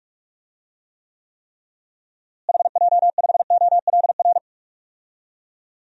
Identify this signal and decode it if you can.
Morse code, heard as a keyed tone, about 43 wpm, 700 Hz. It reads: HJ5OLR
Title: DXCC-100 Call Signs - 6 Characters - Repeated 3x - 45wpm